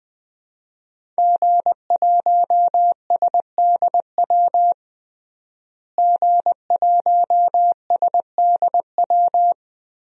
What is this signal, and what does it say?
Morse code, heard as a keyed tone, about 20 wpm, 700 Hz. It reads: G1SDW G1SDW
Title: DXCC-100 Call Signs - 5 Characters - Repeated 3x - 20wpm